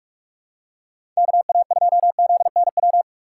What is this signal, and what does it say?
Morse code, heard as a keyed tone, about 46 wpm, 690 Hz. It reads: KA2ZNW
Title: USA Call Signs - 6 Characters - 3x Character Spacing - 45wpm